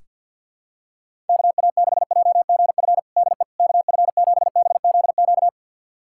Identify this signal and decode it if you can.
Morse code, heard as a keyed tone, about 47 wpm, 700 Hz. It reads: KA6JGF DE KF7BZQ
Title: USA Callsign DE Callsign - 6 Characters - 50wpm